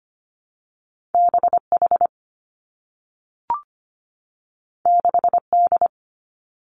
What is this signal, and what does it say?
Morse code, heard as a keyed tone, about 25 wpm, 700 Hz. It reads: B H 6 D
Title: Two Challenging Letter-Number Combinations 25wpm